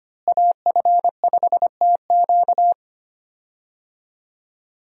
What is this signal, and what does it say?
Morse code, heard as a keyed tone, about 25 wpm, 700 Hz. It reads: AF5TQ
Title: USA Call Signs - 5 Characters - Repeated 2x - 25wpm